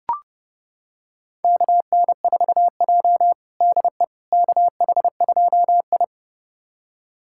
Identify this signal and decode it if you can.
Morse code, heard as a keyed tone, about 30 wpm, 700 Hz. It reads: KN4J DE KH2I
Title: USA Callsign DE Callsign - 4 Characters - 30wpm